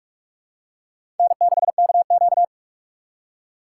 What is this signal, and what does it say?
Morse code, heard as a keyed tone, about 41 wpm, 700 Hz. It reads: N6KQ
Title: USA Call Signs - 4 Characters - 4x Character Spacing - 45wpm